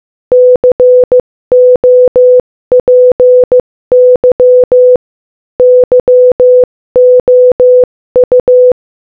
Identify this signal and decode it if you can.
Morse code, heard as a keyed tone, about 15 wpm, 500 Hz. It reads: COPY YOU
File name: Morse-Code-Audio-Speed-15-4